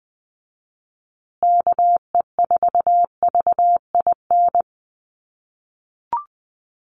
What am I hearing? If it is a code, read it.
Morse code, heard as a keyed tone, about 20 wpm, 700 Hz. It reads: KE4VIN
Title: USA Call Signs Repeated 3x 20wpm